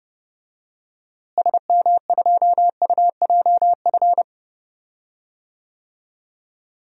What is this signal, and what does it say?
Morse code, heard as a keyed tone, about 30 wpm, 700 Hz. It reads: SM2UJF
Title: DXCC-100 Call Signs - 6 Characters - Repeated 2x - 30wpm